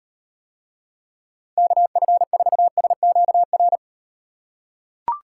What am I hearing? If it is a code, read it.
Morse code, heard as a keyed tone, about 38 wpm, 700 Hz. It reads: KF4SQR